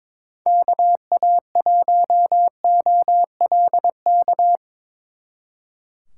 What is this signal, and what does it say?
Morse code, heard as a keyed tone, about 24 wpm, 700 Hz. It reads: KA1OLK
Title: Speed Racing - USA 2x3 Call Signs 22wpm